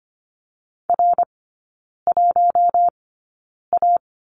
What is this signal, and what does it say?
Morse code, heard as a keyed tone, about 25 wpm, spaced as for 9 wpm, 700 Hz. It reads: R1A
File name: Lesson 17e - Letter u - Three Characters 25wpm